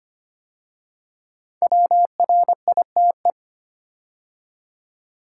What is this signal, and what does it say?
Morse code, heard as a keyed tone, about 25 wpm, 700 Hz. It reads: WRITE
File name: 5 Letter Words - First 3 Letters Given - 25wpm